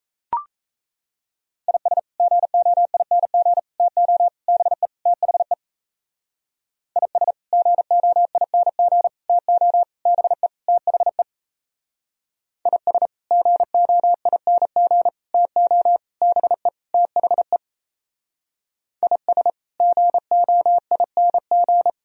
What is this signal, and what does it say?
Morse code, heard as a keyed tone, about 34 wpm, 700 Hz. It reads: IS GOING TO BE THE IS GOING TO BE THE IS GOING TO BE THE IS GOING
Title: Speed-Racing - Sets of 5 Words 28wpm